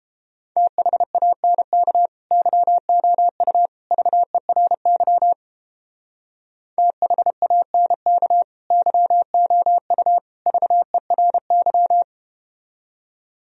Morse code, heard as a keyed tone, about 31 wpm, 700 Hz. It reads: THANK YOU VERY THANK YOU VERY